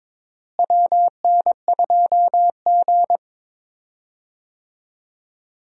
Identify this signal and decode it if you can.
Morse code, heard as a keyed tone, about 22 wpm, 700 Hz. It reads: WN2G
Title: USA Call Signs - 4 Characters - Repeated 2x - 22wpm